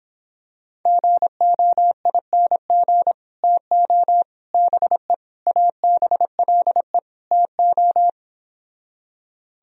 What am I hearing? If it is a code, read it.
Morse code, heard as a keyed tone, about 26 wpm, 700 Hz. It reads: GOING TO BE ABLE TO